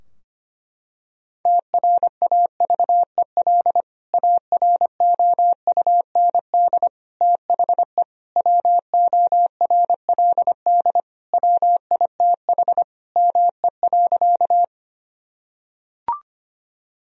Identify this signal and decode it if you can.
Morse code, heard as a keyed tone, about 25 wpm, 700 Hz. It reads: TRAVEL AROUND THE WORLD WITH ME.
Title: Sentences from Top 1000 Words - Review Part 3 - 25wpm